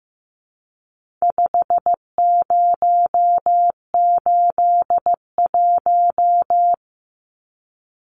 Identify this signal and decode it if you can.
Morse code, heard as a keyed tone, about 15 wpm, 700 Hz. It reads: 5081
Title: Numbers - 4-Digit 15wpm